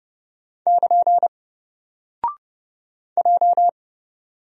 Morse code, heard as a keyed tone, about 30 wpm, 700 Hz. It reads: <KN> J
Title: Single Character (Letter:Number:Punctuation:Prosign) 30wpm